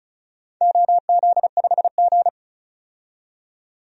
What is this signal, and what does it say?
Morse code, heard as a keyed tone, about 36 wpm, 700 Hz. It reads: OZ5G
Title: DXCC-100 Call Signs - 4 Characters - 35wpm